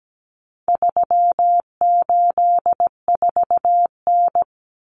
Morse code, heard as a keyed tone, about 17 wpm, 700 Hz. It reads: 384N
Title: Common Cut Numbers - 4-Digit 17wpm